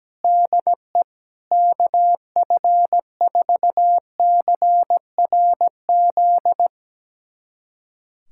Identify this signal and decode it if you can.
Morse code, heard as a keyed tone, about 17 wpm, 700 Hz. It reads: DE KF4CRZ
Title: USA Callsign DE Callsign - 6 Characters - 17wpm